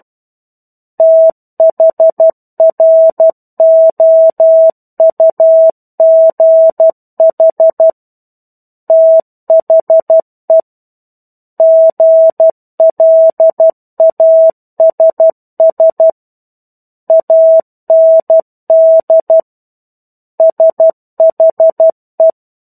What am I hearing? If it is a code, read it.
Morse code, heard as a keyed tone, about 12 wpm, 650 Hz. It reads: THROUGH THE GLASS AND SHE
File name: 01-Chapter-01 12 WPM